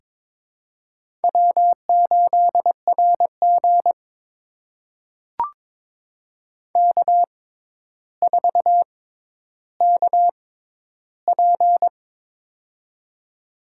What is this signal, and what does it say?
Morse code, heard as a keyed tone, about 22 wpm, spaced as for 11 wpm, 700 Hz. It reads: W8RG K4KP
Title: USA Call Signs - 4 Characters - 6x Character Spacing - 22wpm